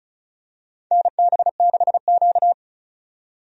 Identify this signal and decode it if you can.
Morse code, heard as a keyed tone, about 36 wpm, 700 Hz. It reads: NB6Q
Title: USA Call Signs - 4 Characters - 35wpm